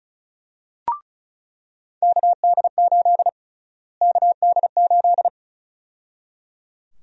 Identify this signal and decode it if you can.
Morse code, heard as a keyed tone, about 34 wpm, 700 Hz. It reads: KD8 KD8
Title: USA Call Sign Prefixes - Repeated 2x - 35wpm